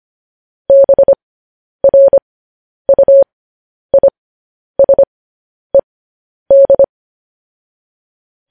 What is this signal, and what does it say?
Morse code, heard as a keyed tone, about 25 wpm, spaced as for 10 wpm, 550 Hz. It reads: BRUISED
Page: news headlines in morse code at 10 wpm